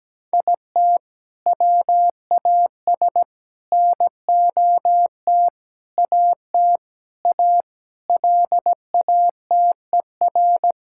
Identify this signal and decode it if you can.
Morse code, heard as a keyed tone, about 17 wpm, 700 Hz. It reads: IT WAS NOT AT A LATER
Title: Sentences from Top 1000 Words - Review Part 5 - 17wpm